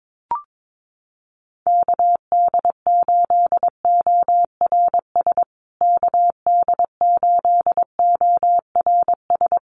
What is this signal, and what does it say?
Morse code, heard as a keyed tone, about 22 wpm, 700 Hz. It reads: KD8ORS KD8ORS
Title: USA Call Signs Repeated 2x 22wpm